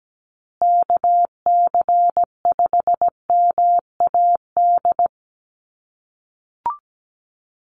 Morse code, heard as a keyed tone, about 17 wpm, 700 Hz. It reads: KC5MAD
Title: USA Call Signs Repeated 3x 17wpm